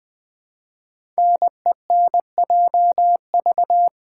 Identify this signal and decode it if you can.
Morse code, heard as a keyed tone, about 20 wpm, 700 Hz. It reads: NENJV
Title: Code Groups - 5 Letters - 20wpm